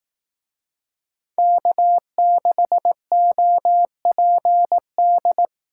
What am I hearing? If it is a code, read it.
Morse code, heard as a keyed tone, about 18 wpm, 700 Hz. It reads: K6OPD
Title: Speed Racing - USA 1x3 Call Signs 15wpm